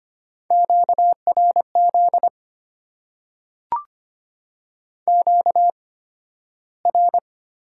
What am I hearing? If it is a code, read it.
Morse code, heard as a keyed tone, about 25 wpm, spaced as for 13 wpm, 690 Hz. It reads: QRZ QR
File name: Top 90 QSO Elements - 8x Character Spacing - 25wpm